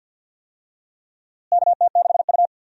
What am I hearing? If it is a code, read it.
Morse code, heard as a keyed tone, about 52 wpm, 700 Hz. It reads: KT6U
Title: USA Call Signs - 4 Characters - Repeated 2x - 50wpm